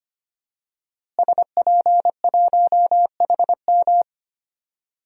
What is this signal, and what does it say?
Morse code, heard as a keyed tone, about 25 wpm, 700 Hz. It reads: SP1HM
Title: DXCC-100 Call Signs - 5 Characters - Repeated 2x - 25wpm